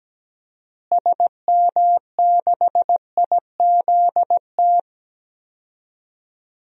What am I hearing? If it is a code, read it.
Morse code, heard as a keyed tone, about 17 wpm, 700 Hz. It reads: SM6IZT
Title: DXCC-100 Call Signs - 6 Characters - Repeated 3x - 17wpm